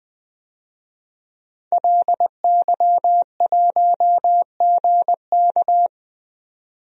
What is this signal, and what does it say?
Morse code, heard as a keyed tone, about 20 wpm, 700 Hz. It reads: LY1GK
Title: DXCC-100 Call Signs - 5 Characters - 20wpm